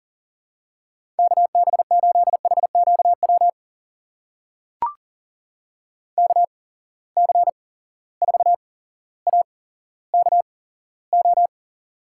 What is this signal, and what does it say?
Morse code, heard as a keyed tone, about 40 wpm, spaced as for 17 wpm, 700 Hz. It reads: KB8HQW KC4AKO
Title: USA Call Signs - 6 Characters - 8x Character Spacing - 40wpm